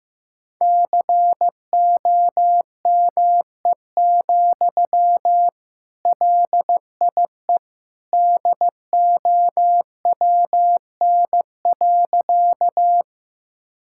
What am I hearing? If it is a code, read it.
Morse code, heard as a keyed tone, about 15 wpm, 700 Hz. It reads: COME, LIE DOWN.
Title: Sentences from Top 1600 Words 15wpm